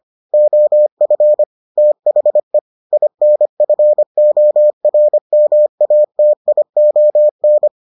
Morse code, heard as a keyed tone, about 25 wpm, 600 Hz. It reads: OF THE INFORMATION